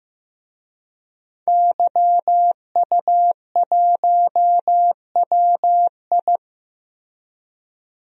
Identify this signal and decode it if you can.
Morse code, heard as a keyed tone, about 16 wpm, 700 Hz. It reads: YU1WI
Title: DXCC-100 Call Signs - 5 Characters - Repeated 2x - 15wpm